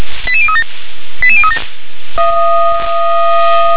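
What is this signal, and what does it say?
Five-tone selective call (ZVEI), heard as quick runs of five tones, tones of about 70 ms. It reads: 88037 88037
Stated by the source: ZVEI five-tone selective call 88162 received by SDR.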